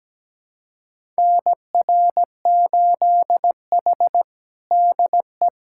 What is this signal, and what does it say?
Morse code, heard as a keyed tone, about 17 wpm, 700 Hz. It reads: NR8H DE
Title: USA Callsign DE Callsign - 4 Characters - 17wpm